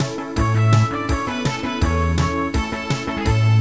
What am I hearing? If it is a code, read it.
There is no code.